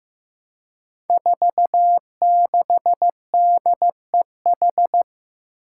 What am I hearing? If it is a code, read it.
Morse code, heard as a keyed tone, about 15 wpm, 700 Hz. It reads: 46DEH
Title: Five Challenging Letter-Number Combinations 15wpm